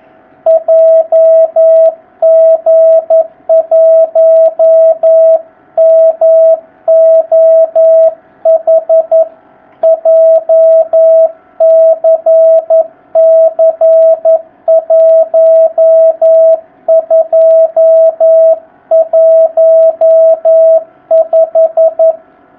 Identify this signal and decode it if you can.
Morse code, heard as a keyed tone, about 11 wpm, 650 Hz. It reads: JG1MOH JCC1215
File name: jh8cbh / 2019cw